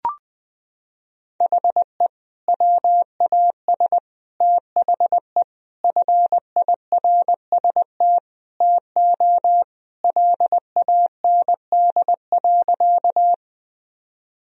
Morse code, heard as a keyed tone, about 20 wpm, 700 Hz. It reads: HE WAS THE FIRST TO LAND.
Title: Sentences from Top 1000 Words - Review Part 1 - 20wpm